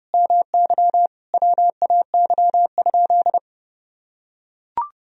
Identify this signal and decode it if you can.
Morse code, heard as a keyed tone, about 32 wpm, 700 Hz. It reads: MY WAY?
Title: Sentences from Top 200 Words and Abbreviations 30wpm